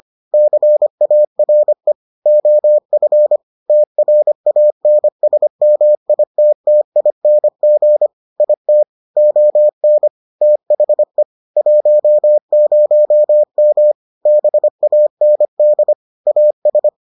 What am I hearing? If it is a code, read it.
Morse code, heard as a keyed tone, about 25 wpm, 600 Hz. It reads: CARE OF TRANSMITTING IT ON THE 10M BAND AS